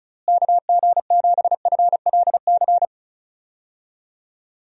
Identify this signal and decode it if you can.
Morse code, heard as a keyed tone, about 36 wpm, 700 Hz. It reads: KG7FLC
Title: USA Call Signs - 6 Characters - Repeated 2x - 35wpm